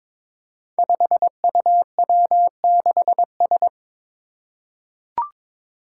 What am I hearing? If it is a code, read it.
Morse code, heard as a keyed tone, about 22 wpm, 700 Hz. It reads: 5UW6S
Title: Five Challenging Letter-Number Combinations 22wpm